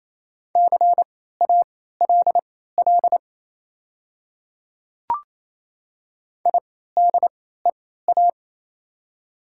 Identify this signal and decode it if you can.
Morse code, heard as a keyed tone, about 28 wpm, spaced as for 16 wpm, 700 Hz. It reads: CALL IDEA
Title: 3x Character Spacing - Top 300 Words - 28wpm